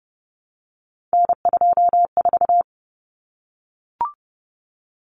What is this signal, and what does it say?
Morse code, heard as a keyed tone, about 30 wpm, 700 Hz. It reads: N24
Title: Common Cut Numbers - 3-Digit 30wpm